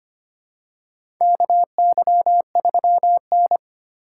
Morse code, heard as a keyed tone, about 25 wpm, 700 Hz. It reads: KY3N
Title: USA Call Signs - 4 Characters - 2x Character Spacing - 25wpm